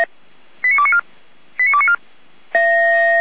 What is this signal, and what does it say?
Five-tone selective call (ZVEI), heard as quick runs of five tones, tones of about 70 ms. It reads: 89294 89294